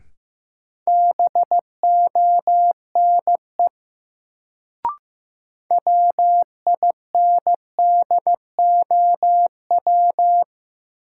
Morse code, heard as a keyed tone, about 15 wpm, 700 Hz. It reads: BONE WINDOW
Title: Sentences from Top 1600 Words - New Word Drill Alt - 15wpm